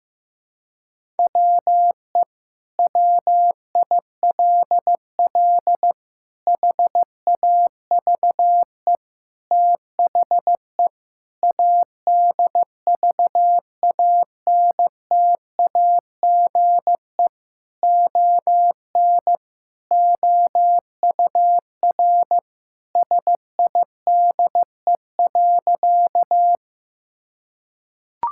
Morse code, heard as a keyed tone, about 15 wpm, 700 Hz. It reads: WE WILL HAVE THE ADVANTAGE ON OUR SIDE.
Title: Sentences from Top 1000 Words - Review Part 6 - 15wpm